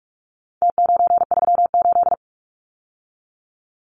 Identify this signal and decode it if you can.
Morse code, heard as a keyed tone, about 44 wpm, 700 Hz. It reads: T938
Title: Numbers - Contest Style - 4-Digit 45wpm